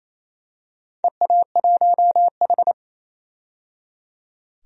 Morse code, heard as a keyed tone, about 28 wpm, 700 Hz. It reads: EA1H
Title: DXCC-100 Call Signs - 4 Characters - 28wpm